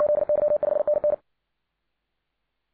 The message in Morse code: ZY5NN